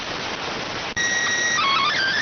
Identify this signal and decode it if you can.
SSTV (slow-scan television), PD 290.